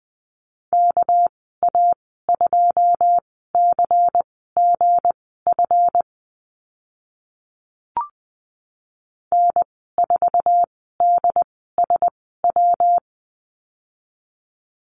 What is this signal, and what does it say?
Morse code, heard as a keyed tone, about 20 wpm, spaced as for 15 wpm, 700 Hz. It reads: KA2CGF N4DSW
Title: 231 Lesson 37g - Letter x - Call Signs 20wpm